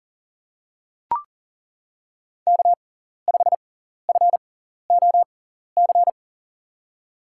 Morse code, heard as a keyed tone, about 40 wpm, spaced as for 14 wpm, 700 Hz. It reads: K5FOC
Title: USA Call Signs - 5 Characters - 6x Character Spacing - 40wpm